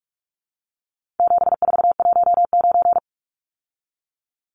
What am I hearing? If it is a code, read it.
Morse code, heard as a keyed tone, about 44 wpm, 700 Hz. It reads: 7419
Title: Numbers - 4-Digit 45wpm